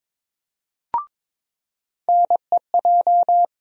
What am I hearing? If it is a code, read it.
Morse code, heard as a keyed tone, about 22 wpm, 700 Hz. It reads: NEJ
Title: USA Call Sign Postfixes - 3 Letters 22wpm